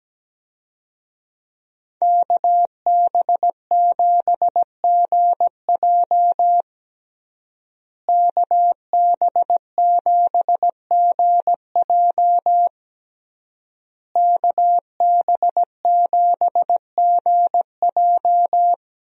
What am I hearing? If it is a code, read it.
Morse code, heard as a keyed tone, about 17 wpm, 700 Hz. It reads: KB7GJ KB7GJ KB7GJ
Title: USA Call Signs - 5 Characters - Repeated 3x - 17wpm